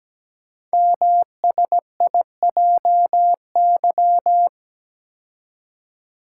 Morse code, heard as a keyed tone, about 17 wpm, 700 Hz. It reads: MSIJY
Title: Code Groups - 5 Letters - 17wpm